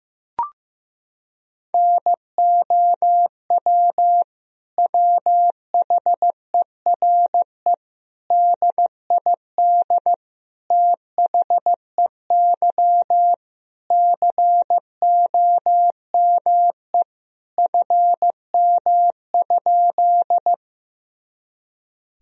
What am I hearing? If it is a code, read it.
Morse code, heard as a keyed tone, about 15 wpm, 700 Hz. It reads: NOW WHERE DID THEY COME FM?
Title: Sentences from Top 200 Words and Abbreviations 15wpm